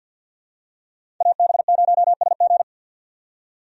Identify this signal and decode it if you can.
Morse code, heard as a keyed tone, about 50 wpm, 700 Hz. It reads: AB0SG
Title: USA Call Signs - 5 Characters - 50wpm